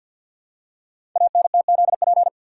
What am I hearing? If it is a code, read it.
Morse code, heard as a keyed tone, about 49 wpm, 700 Hz. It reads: ANTZP
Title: Code Groups - 5 Letters - 50wpm